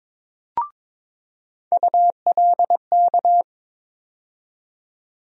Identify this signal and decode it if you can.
Morse code, heard as a keyed tone, about 22 wpm, 700 Hz. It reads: ULK
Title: USA Call Sign Postfixes - 3 Letters 22wpm